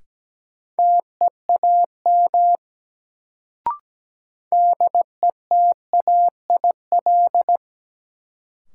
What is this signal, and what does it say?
Morse code, heard as a keyed tone, about 17 wpm, 700 Hz. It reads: TEAM DETAIL
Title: Sentences from Top 1500 Words - New Word Drill - 17wpm